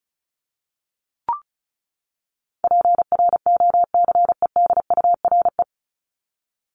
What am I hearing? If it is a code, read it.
Morse code, heard as a keyed tone, about 34 wpm, 700 Hz. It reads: PROCEDURE
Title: Top 1751-2000 Words 35wpm